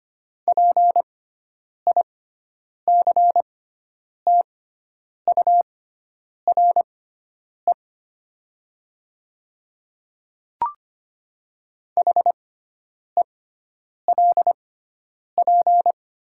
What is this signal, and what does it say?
Morse code, heard as a keyed tone, about 25 wpm, spaced as for 9 wpm, 700 Hz. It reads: PICTURE HELP
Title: 6x Character Spacing - Top 300 Words - 25wpm